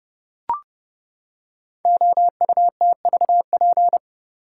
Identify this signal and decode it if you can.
Morse code, heard as a keyed tone, about 31 wpm, 700 Hz. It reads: OUTVP
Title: Code Groups - 5 Letter:Numbers - 30wpm